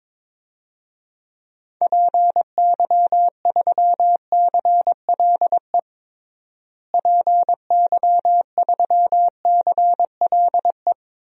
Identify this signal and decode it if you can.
Morse code, heard as a keyed tone, about 22 wpm, 700 Hz. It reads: PY3CLE PY3CLE
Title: DXCC-100 Call Signs - 6 Characters - Repeated 2x - 22wpm